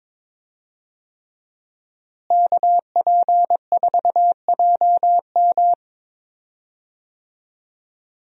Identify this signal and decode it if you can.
Morse code, heard as a keyed tone, about 22 wpm, 700 Hz. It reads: KP4JM
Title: DXCC-100 Call Signs - 5 Characters - Repeated 3x - 22wpm